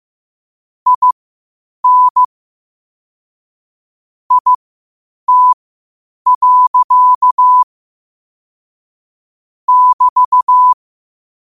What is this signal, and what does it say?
Morse code, heard as a keyed tone, about 15 wpm, spaced as for 8 wpm, 1000 Hz. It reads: IN IT. <BT>
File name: qotdfiles / 8wpm